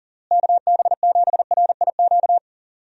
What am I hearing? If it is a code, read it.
Morse code, heard as a keyed tone, about 40 wpm, 700 Hz. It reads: KB7RIQ
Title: USA Call Signs - 6 Characters - Repeated 3x - 40wpm